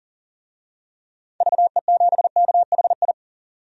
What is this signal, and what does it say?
Morse code, heard as a keyed tone, about 40 wpm, 700 Hz. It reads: VE7KHI
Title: DXCC-100 Call Signs - 6 Characters - 40wpm